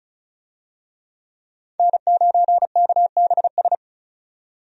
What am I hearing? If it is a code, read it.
Morse code, heard as a keyed tone, about 36 wpm, 700 Hz. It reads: N9KBS
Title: USA Call Signs - 5 Characters - 35wpm